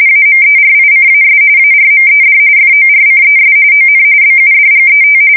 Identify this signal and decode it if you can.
RTTY, about 45.45 baud. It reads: AMERICAS ARMED FORCES ON BUDGET
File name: fsk170 / battleshipnj